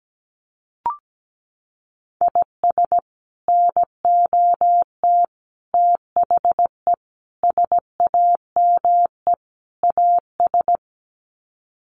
Morse code, heard as a keyed tone, about 17 wpm, 700 Hz. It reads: IS NOT THE SAME AS
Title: Sets of 5 Words Encore 17wpm